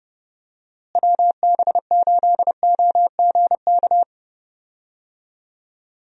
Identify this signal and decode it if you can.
Morse code, heard as a keyed tone, about 32 wpm, 700 Hz. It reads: WB8OGK
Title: USA Call Signs - 6 Characters - Repeated 3x - 30wpm